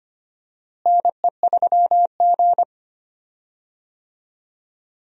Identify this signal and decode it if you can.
Morse code, heard as a keyed tone, about 25 wpm, 700 Hz. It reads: NE3G
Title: USA Call Signs - 4 Characters - Repeated 3x - 25wpm